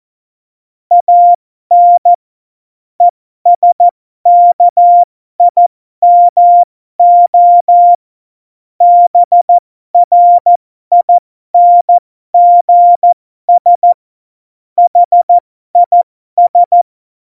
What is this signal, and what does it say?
Morse code, heard as a keyed tone, about 14 wpm, spaced as for 12 wpm, 700 Hz. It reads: AN ESKIMO BRINGS HIS
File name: Morsekurs / Witze